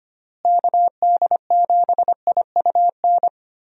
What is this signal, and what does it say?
Morse code, heard as a keyed tone, about 25 wpm, 700 Hz. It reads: KD7IUN
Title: USA Call Signs - 6 Characters - Repeated 3x - 25wpm